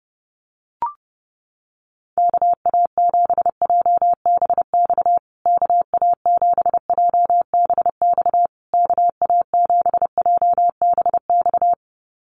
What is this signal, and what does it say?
Morse code, heard as a keyed tone, about 30 wpm, 700 Hz. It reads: KA7JBX KA7JBX KA7JBX
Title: USA Call Signs Repeated 3x 30wpm